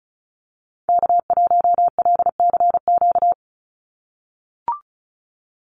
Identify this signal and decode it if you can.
Morse code, heard as a keyed tone, about 34 wpm, 700 Hz. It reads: K1LCQ
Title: USA Call Signs 35wpm